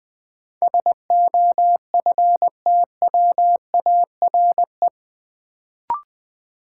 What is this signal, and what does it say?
Morse code, heard as a keyed tone, about 20 wpm, 700 Hz. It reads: SOFTWARE
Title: Sentences from Top 1000 Words - Review Part 3 - New Word Drill - 20wpm